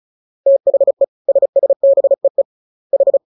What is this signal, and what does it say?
Morse code, heard as a keyed tone, about 34 wpm, 550 Hz. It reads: THE SSBEE H
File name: jokes / joke 26 35 0000